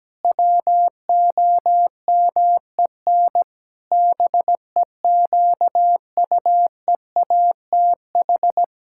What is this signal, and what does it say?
Morse code, heard as a keyed tone, about 17 wpm, 700 Hz. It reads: WOMEN BEQUEATH
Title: Pangram Challenge v2 - 17wpm